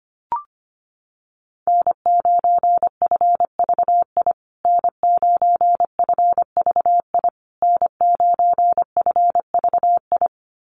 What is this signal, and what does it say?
Morse code, heard as a keyed tone, about 25 wpm, 700 Hz. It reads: N9FVI N9FVI N9FVI
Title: USA Call Signs Repeated 3x 25wpm